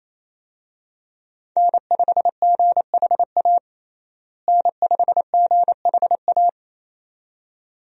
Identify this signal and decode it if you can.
Morse code, heard as a keyed tone, about 28 wpm, 700 Hz. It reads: N5GHA N5GHA
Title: USA Call Signs - 5 Characters - Repeated 2x - 28wpm